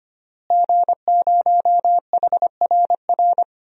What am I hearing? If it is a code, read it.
Morse code, heard as a keyed tone, about 25 wpm, 700 Hz. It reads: G0HRR
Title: DXCC-100 Call Signs - 5 Characters - Repeated 3x - 25wpm